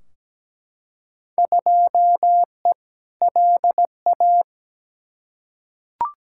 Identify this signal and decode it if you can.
Morse code, heard as a keyed tone, about 17 wpm, 700 Hz. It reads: 2E LA
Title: Field Day Exchanges Encore - 17wpm